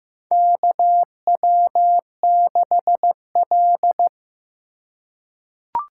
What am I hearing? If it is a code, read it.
Morse code, heard as a keyed tone, about 15 wpm, 700 Hz. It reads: KW6L
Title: USA Call Signs - 4 Characters - 4x Character Spacing - 15wpm